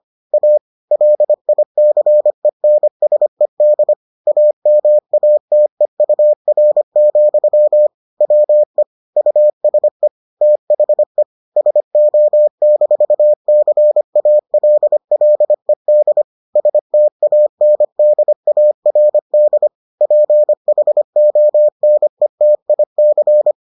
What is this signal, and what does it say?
Morse code, heard as a keyed tone, about 25 wpm, 600 Hz. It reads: A LICENSED AMATEUR, WE USE THE SO-CALLED STANDARD PHONETIC